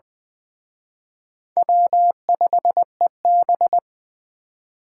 Morse code, heard as a keyed tone, about 20 wpm, 700 Hz. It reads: W5EB
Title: Speed Racing - USA Call Signs 17wpm